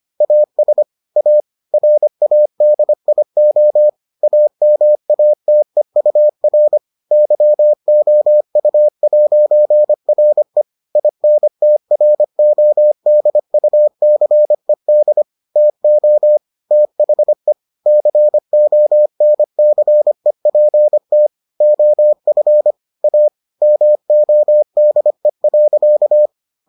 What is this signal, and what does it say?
Morse code, heard as a keyed tone, about 25 wpm, 600 Hz. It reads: AS A RADIO AMATEUR YOU'RE INTRODUCED TO THE CONCEPT OF A MODE.